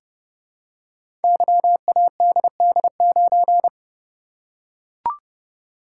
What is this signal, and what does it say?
Morse code, heard as a keyed tone, about 29 wpm, 700 Hz. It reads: YADD9
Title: Code Groups - 5 Letter:Numbers - 30wpm